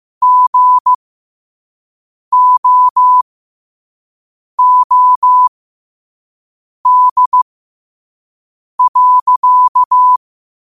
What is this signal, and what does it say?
Morse code, heard as a keyed tone, about 15 wpm, spaced as for 5 wpm, 1000 Hz. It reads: GOOD.
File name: qotdfiles / 5wpm